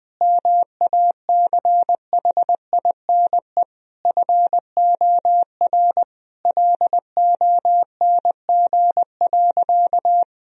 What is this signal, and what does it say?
Morse code, heard as a keyed tone, about 20 wpm, 700 Hz. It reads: MACHINE FOR LONG.
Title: Sentences from Top 1000 Words - Review Part 2 - 20wpm